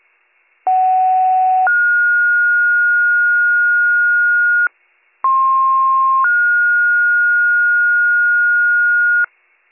Two-tone page, heard as a one-second tone, then a three-second tone, about 750 then 1430 Hz, about 1060 then 1430 Hz. It is heard from 0.7 to 9.3 s.